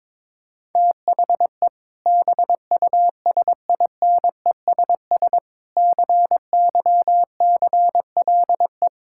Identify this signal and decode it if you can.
Morse code, heard as a keyed tone, about 22 wpm, 700 Hz. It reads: THE BUSINESS CYCLE
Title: Sentences from Top 1000 Words - Review Part 6 - 22wpm